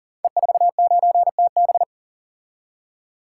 Morse code, heard as a keyed tone, about 40 wpm, 700 Hz. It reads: E49TB
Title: Code Groups - 5 Letter:Numbers - 40wpm